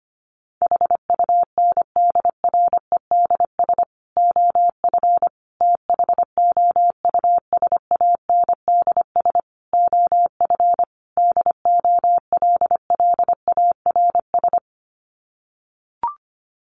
Morse code, heard as a keyed tone, about 25 wpm, 700 Hz. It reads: HUNDREDS OF THOUSANDS OF DOLLARS
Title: Sets of 5 Words Encore 25wpm